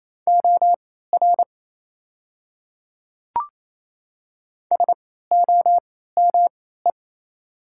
Morse code, heard as a keyed tone, about 27 wpm, spaced as for 16 wpm, 690 Hz. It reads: OR SOME
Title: 3x Character Spacing - Top 100 Words - 28wpm